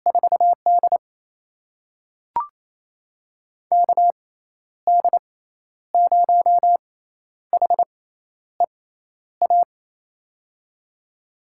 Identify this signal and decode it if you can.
Morse code, heard as a keyed tone, about 27 wpm, spaced as for 11 wpm, 700 Hz. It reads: 4D KD0HEA
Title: USA Call Signs - 6x Character Spacing - 28wpm